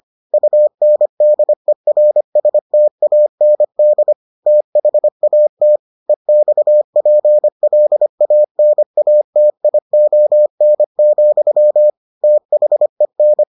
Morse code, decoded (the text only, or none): UNDERSTAND THAT EXPLANATION, THEN